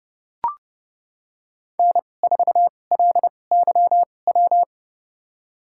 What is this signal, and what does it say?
Morse code, heard as a keyed tone, about 29 wpm, spaced as for 21 wpm, 690 Hz. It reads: N4LYW
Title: USA Call Signs - 5 Characters - 2x Character Spacing - 30wpm